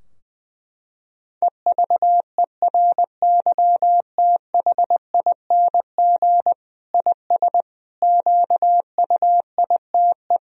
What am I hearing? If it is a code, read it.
Morse code, heard as a keyed tone, about 20 wpm, 700 Hz. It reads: EVERYTHING IS QUITE
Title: Sentences from Top 1000 Words - Review Part 2 - 20wpm